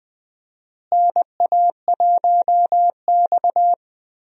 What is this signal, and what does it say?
Morse code, heard as a keyed tone, about 20 wpm, 700 Hz. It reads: NA1X
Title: USA Call Signs - 4 Characters - Repeated 2x - 20wpm